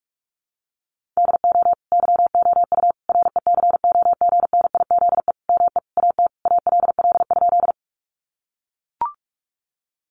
Morse code, heard as a keyed tone, about 44 wpm, 700 Hz. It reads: DO YOU RECOGNIZE ME AT ALL?
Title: Sentences from Top 1400 Words 45wpm